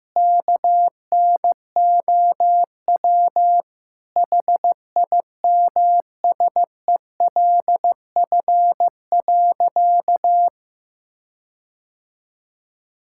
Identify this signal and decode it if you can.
Morse code, heard as a keyed tone, about 16 wpm, 700 Hz. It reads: KNOW HIMSELF.
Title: Sentences from Top 1600 Words 15wpm